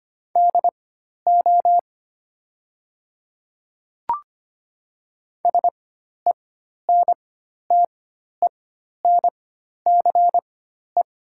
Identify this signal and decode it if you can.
Morse code, heard as a keyed tone, about 25 wpm, spaced as for 12 wpm, 690 Hz. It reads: DO SENTENCE
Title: 4x Character Spacing - Top 200 Words - 25wpm